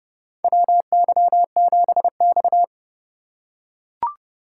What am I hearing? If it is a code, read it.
Morse code, heard as a keyed tone, about 30 wpm, 700 Hz. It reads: WY7X
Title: USA Call Signs - 4 Characters - 3x Character Spacing - 30wpm